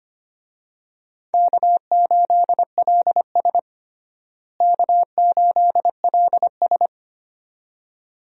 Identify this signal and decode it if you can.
Morse code, heard as a keyed tone, about 25 wpm, 700 Hz. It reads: K8LS K8LS
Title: USA Call Signs - 4 Characters - Repeated 2x - 25wpm